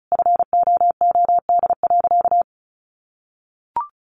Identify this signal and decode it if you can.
Morse code, heard as a keyed tone, about 34 wpm, 700 Hz. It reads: FOOD.
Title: Sentences from Top 1500 Words 35wpm